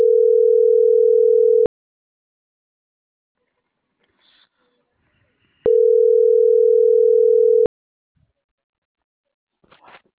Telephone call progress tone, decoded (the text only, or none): ringback tone